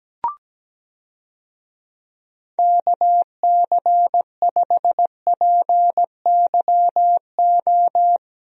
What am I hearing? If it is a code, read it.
Morse code, heard as a keyed tone, about 17 wpm, 700 Hz. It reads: KC5PYO
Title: USA Call Signs - 6 Characters - Repeated 3x - 17wpm